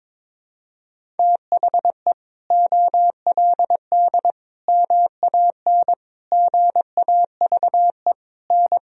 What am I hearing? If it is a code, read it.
Morse code, heard as a keyed tone, about 21 wpm, 700 Hz. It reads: THE OLD MAN GAVE N
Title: Sentences from Top 500 Words and Abbreviations 22wpm